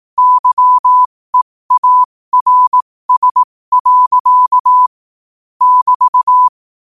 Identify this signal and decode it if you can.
Morse code, heard as a keyed tone, about 18 wpm, spaced as for 15 wpm, 1000 Hz. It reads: YEARS. <BT>
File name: qotdfiles / 15wpm